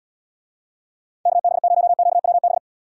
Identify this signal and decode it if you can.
Morse code, heard as a keyed tone, about 78 wpm, 700 Hz. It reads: KB0QWB